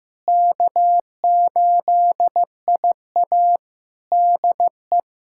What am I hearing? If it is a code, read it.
Morse code, heard as a keyed tone, about 15 wpm, 700 Hz. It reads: K8IA DE
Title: Practice QSO - Exchange 4 - Ending - 15wpm